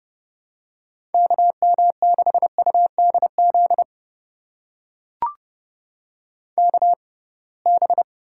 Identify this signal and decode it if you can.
Morse code, heard as a keyed tone, about 30 wpm, spaced as for 23 wpm, 700 Hz. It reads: KM6UDZ KB
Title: USA Call Signs - 6 Characters - 6x Character Spacing - 30wpm